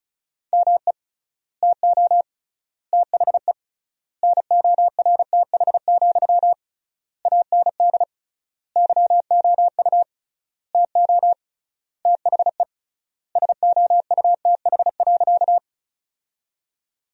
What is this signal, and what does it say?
Morse code, heard as a keyed tone, about 34 wpm, 700 Hz. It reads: ME TO THE NORTH, AND YOU TO THE SOUTH.
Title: 3x Word-Spacing - Sentences from Top 400 Words - 35wpm